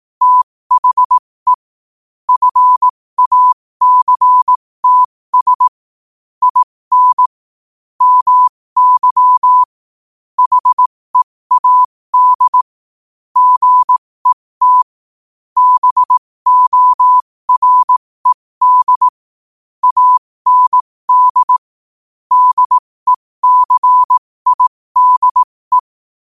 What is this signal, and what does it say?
Morse code, heard as a keyed tone, about 18 wpm, spaced as for 15 wpm, 1000 Hz. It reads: THE FACTS IN MY HEAD GET BORED AND DECIDE
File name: qotdfiles / 15wpm